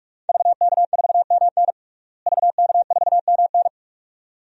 Morse code, heard as a keyed tone, about 47 wpm, 700 Hz. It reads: VK4MN VK4MN
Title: DXCC-100 Call Signs - 5 Characters - Repeated 2x - 45wpm